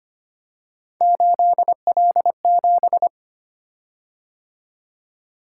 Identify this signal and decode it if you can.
Morse code, heard as a keyed tone, about 25 wpm, 700 Hz. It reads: 8L7
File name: Three Letter-Number Combinations - 25wpm